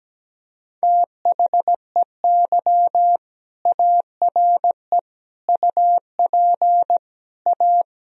Morse code, heard as a keyed tone, about 17 wpm, 700 Hz. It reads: THEY ARE UP A
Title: Sentences from Top 1000 Words - Review Part 1 - 17wpm